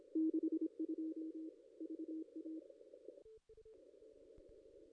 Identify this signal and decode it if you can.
Morse code, heard as a keyed tone, about 26 wpm, 320 Hz. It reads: 62 VA
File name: contests / ss-cw-k5tr-op-n5rz -1600